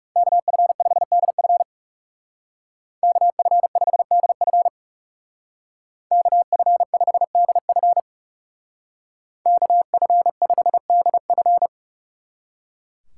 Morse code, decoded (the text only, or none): KF5DF KF5DF KF5DF KF5DF